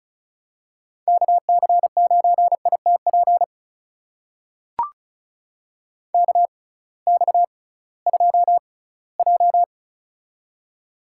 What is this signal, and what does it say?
Morse code, heard as a keyed tone, about 34 wpm, spaced as for 20 wpm, 700 Hz. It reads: KC9ITP KX2J
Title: USA Call Signs - 6x Character Spacing - 35wpm